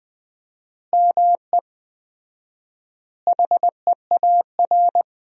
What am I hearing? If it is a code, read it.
Morse code, heard as a keyed tone, about 20 wpm, 700 Hz. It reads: ME HEAR
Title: 4x Word-Spacing - Sentences from Top 400 Words - 20wpm